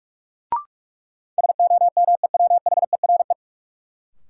Morse code, heard as a keyed tone, about 46 wpm, 700 Hz. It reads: SOMEWHERE
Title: Sentences from Top 1400 Words - New Word Drill Alt - 45wpm